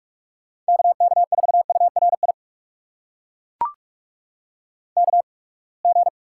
Morse code, heard as a keyed tone, about 42 wpm, spaced as for 30 wpm, 700 Hz. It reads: KK4URI KG
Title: USA Call Signs - 6 Characters - 8x Character Spacing - 45wpm